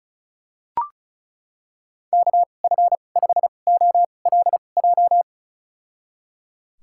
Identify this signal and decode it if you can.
Morse code, heard as a keyed tone, about 34 wpm, spaced as for 25 wpm, 700 Hz. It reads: KF5OLJ
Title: USA Call Signs - 6 Characters - 2x Character Spacing - 35wpm